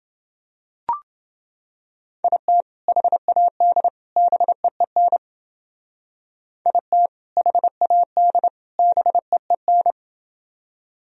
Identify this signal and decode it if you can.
Morse code, heard as a keyed tone, about 28 wpm, 700 Hz. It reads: IT HAD BEEN IT HAD BEEN